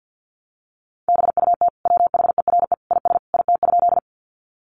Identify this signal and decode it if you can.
Morse code, heard as a keyed tone, about 50 wpm, 700 Hz. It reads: BUT WHERE IS IT?
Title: Sentences from Top 200 Words 50wpm